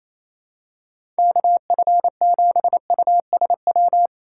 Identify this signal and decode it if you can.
Morse code, heard as a keyed tone, about 28 wpm, 700 Hz. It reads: KF7USW
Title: USA Call Signs - 6 Characters - 4x Character Spacing - 28wpm